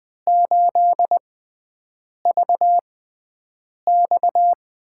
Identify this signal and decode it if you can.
Morse code, heard as a keyed tone, about 20 wpm, spaced as for 7 wpm, 700 Hz. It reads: 8VX
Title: USA Call Signs - 5 Characters - 6x Character Spacing - 20wpm